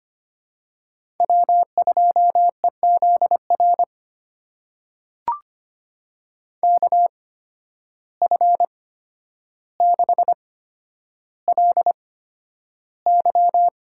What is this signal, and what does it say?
Morse code, heard as a keyed tone, about 25 wpm, spaced as for 11 wpm, 700 Hz. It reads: W2EZR KF6LY
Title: USA Call Signs - 5 Characters - 8x Character Spacing - 25wpm